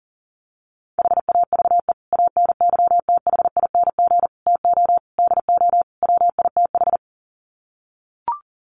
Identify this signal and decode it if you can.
Morse code, heard as a keyed tone, about 40 wpm, 700 Hz. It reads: HAVE ANYTHING TO DO WITH
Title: Sets of 5 Words 40wpm